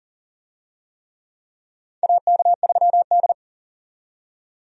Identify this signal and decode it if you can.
Morse code, heard as a keyed tone, about 38 wpm, 700 Hz. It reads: AK3D
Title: USA Call Signs - 4 Characters - 40wpm